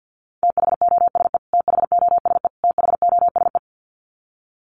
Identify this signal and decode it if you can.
Morse code, heard as a keyed tone, about 50 wpm, 700 Hz. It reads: THOSE THOSE THOSE
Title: Top 500 Words Repeated 50wpm